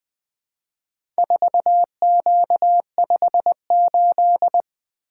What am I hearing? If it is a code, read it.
Morse code, heard as a keyed tone, about 20 wpm, 700 Hz. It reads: 4Q58
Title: Four Letter-Number Combinations - 20wpm